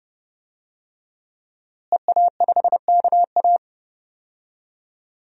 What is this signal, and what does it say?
Morse code, heard as a keyed tone, about 30 wpm, 700 Hz. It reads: EA5KA
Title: DXCC-100 Call Signs - 5 Characters - Repeated 3x - 30wpm